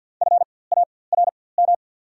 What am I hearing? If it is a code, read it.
Morse code, heard as a keyed tone, about 54 wpm, spaced as for 24 wpm, 690 Hz. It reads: FARM
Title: 4x Character Spacing - Top 500 Words - 50wpm